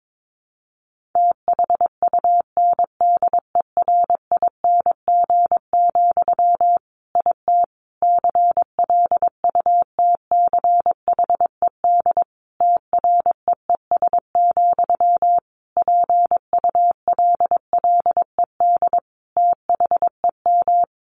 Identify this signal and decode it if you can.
Morse code, heard as a keyed tone, about 22 wpm, 700 Hz. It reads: THUNDERING, IT CLUTCHED TREES, PULLED THEM
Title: A Sound Of Thunder 22wpm